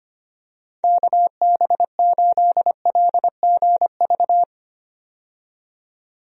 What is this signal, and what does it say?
Morse code, heard as a keyed tone, about 25 wpm, 700 Hz. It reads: KB8LGV
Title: USA Call Signs - 6 Characters - 25wpm